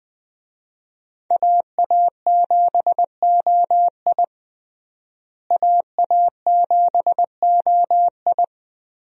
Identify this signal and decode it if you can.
Morse code, heard as a keyed tone, about 20 wpm, 700 Hz. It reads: AA7OI AA7OI
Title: USA Call Signs - 5 Characters - Repeated 3x - 20wpm